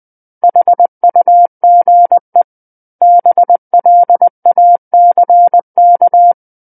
Morse code, heard as a keyed tone, about 20 wpm, 700 Hz. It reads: HUGE BLACK
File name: 20 WPM